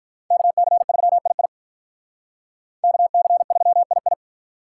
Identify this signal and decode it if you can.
Morse code, heard as a keyed tone, about 45 wpm, 700 Hz. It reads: KC3II KC3II